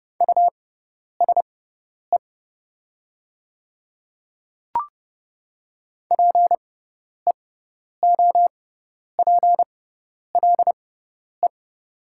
Morse code, heard as a keyed tone, about 30 wpm, spaced as for 10 wpm, 700 Hz. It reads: USE PEOPLE
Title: 6x Character Spacing - Top 300 Words - 30wpm